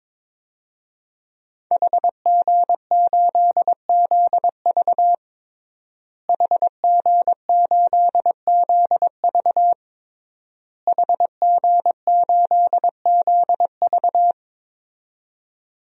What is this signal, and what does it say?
Morse code, heard as a keyed tone, about 22 wpm, 700 Hz. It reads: HG8ZV HG8ZV HG8ZV
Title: DXCC-100 Call Signs - 5 Characters - Repeated 3x - 22wpm